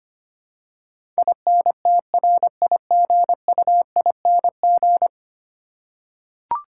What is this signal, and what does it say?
Morse code, heard as a keyed tone, about 25 wpm, 700 Hz. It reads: INTRIGUING
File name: Ten Letter Words - 25wpm